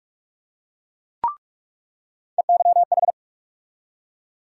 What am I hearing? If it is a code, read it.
Morse code, heard as a keyed tone, about 42 wpm, 700 Hz. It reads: EYH
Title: USA Call Sign Postfixes - 3 Letters 45wpm